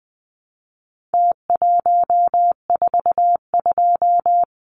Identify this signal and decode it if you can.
Morse code, heard as a keyed tone, about 20 wpm, 700 Hz. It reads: T142
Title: Numbers - Contest Style - 4-Digit 20wpm